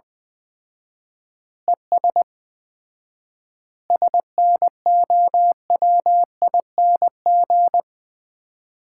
Morse code, heard as a keyed tone, about 21 wpm, 700 Hz. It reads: ES SNOWING
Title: Common QSO Phrases - 4x Word-Spacing - 20wpm